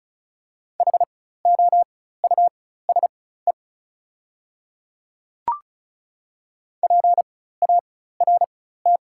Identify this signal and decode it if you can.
Morse code, heard as a keyed tone, about 36 wpm, spaced as for 16 wpm, 690 Hz. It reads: HOUSE PART
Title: 4x Character Spacing - Top 200 Words - 35wpm